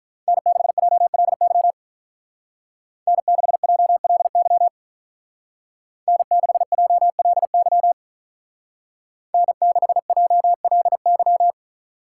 Morse code, heard as a keyed tone, about 47 wpm, 700 Hz. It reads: N6JLY N6JLY N6JLY N6JLY